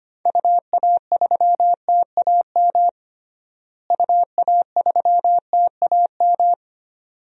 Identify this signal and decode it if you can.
Morse code, heard as a keyed tone, about 25 wpm, 700 Hz. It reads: UA3TAM UA3TAM
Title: DXCC-100 Call Signs - 6 Characters - Repeated 2x - 25wpm